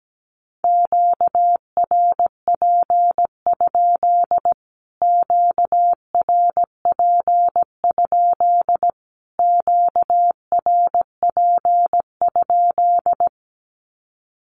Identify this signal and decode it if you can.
Morse code, heard as a keyed tone, about 17 wpm, 700 Hz. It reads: QRP? QRP? QRP?
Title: Q-Codes Repeated 3x 17wpm